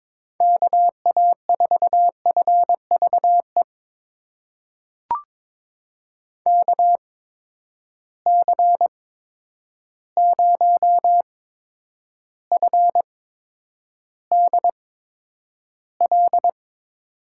Morse code, heard as a keyed tone, about 22 wpm, spaced as for 9 wpm, 700 Hz. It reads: KA4FVE KC0FDL
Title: USA Call Signs - 6 Characters - 8x Character Spacing - 22wpm